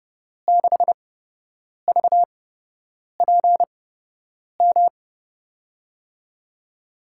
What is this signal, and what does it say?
Morse code, heard as a keyed tone, about 30 wpm, spaced as for 8 wpm, 700 Hz. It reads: 6VPM
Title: USA Call Signs - 6 Characters - 8x Character Spacing - 30wpm